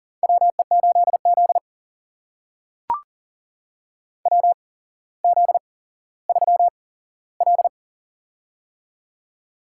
Morse code, heard as a keyed tone, about 42 wpm, spaced as for 17 wpm, 700 Hz. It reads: WE8Z WZ3L
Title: USA Call Signs - 4 Characters - 8x Character Spacing - 40wpm